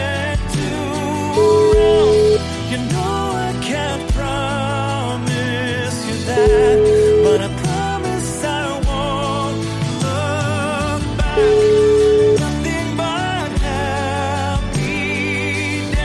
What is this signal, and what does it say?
Ringback tone (Europe, 425 Hz), with noise.